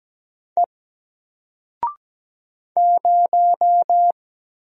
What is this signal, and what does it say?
Morse code, heard as a keyed tone, about 17 wpm, 700 Hz. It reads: E 0
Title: Single Character (Letter:Number:Punctuation:Prosign) Alt 17wpm